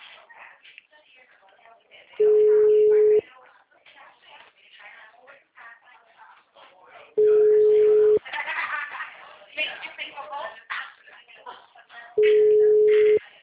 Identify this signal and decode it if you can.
Ringback tone (Europe, 425 Hz), with speech.